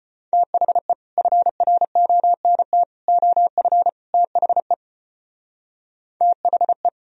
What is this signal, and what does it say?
Morse code, heard as a keyed tone, about 32 wpm, 700 Hz. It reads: THE FRONT OF THE THE